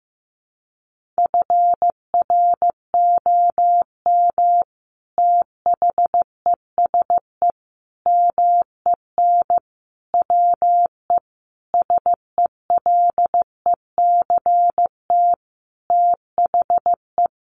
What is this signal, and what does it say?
Morse code, heard as a keyed tone, about 15 wpm, 700 Hz. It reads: FROM THESE MEN WE SELECT THE
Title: Sentences from Top 1300 Words 15wpm